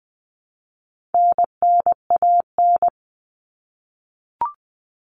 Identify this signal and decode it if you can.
Morse code, heard as a keyed tone, about 20 wpm, 700 Hz. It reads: NNAN
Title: All Cut Numbers - 4-Digit 20wpm